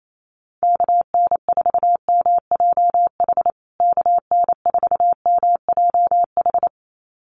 Morse code, heard as a keyed tone, about 28 wpm, 700 Hz. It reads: KN4MJH KN4MJH
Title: USA Call Signs Repeated 2x 28wpm